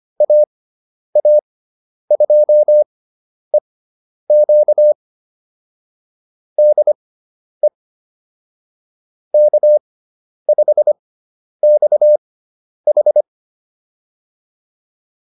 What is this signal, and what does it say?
Morse code, heard as a keyed tone, about 25 wpm, spaced as for 10 wpm, 600 Hz. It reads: AA2EQ DE K5XH
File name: QSO210 10